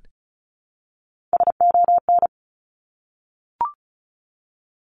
Morse code, heard as a keyed tone, about 36 wpm, 700 Hz. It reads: SON
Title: Top Three Letter Combinations in Words 35wpm